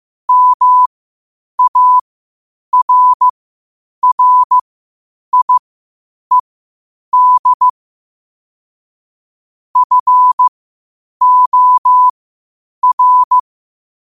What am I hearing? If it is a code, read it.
Morse code, heard as a keyed tone, about 15 wpm, spaced as for 8 wpm, 1000 Hz. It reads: MARRIED FOR
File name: qotdfiles / 8wpm